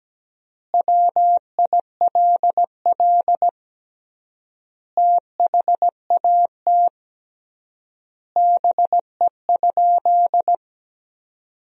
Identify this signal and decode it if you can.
Morse code, heard as a keyed tone, about 17 wpm, 700 Hz. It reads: WILL THAT BE?
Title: 3x Word-Spacing - Sentences from Top 100 Words - 17wpm